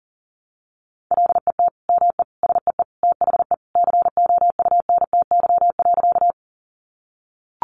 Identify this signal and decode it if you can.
Morse code, heard as a keyed tone, about 37 wpm, 700 Hz. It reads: LET ME SEE THE COUNTY.
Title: Sentences from Top 1200 Words 40wpm